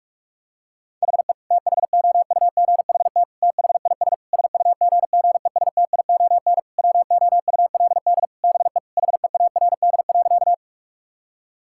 Morse code, heard as a keyed tone, about 46 wpm, 700 Hz. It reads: HE THOUGHT THIS SUGGESTION WOULD BE HEARD.
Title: Sentences from Top 1600 Words 45wpm